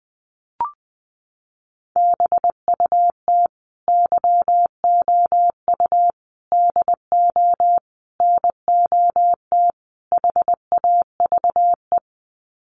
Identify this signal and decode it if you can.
Morse code, heard as a keyed tone, about 19 wpm, 700 Hz. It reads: BUT YOU DO NOT HAVE
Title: Sets of 5 Words Encore 20wpm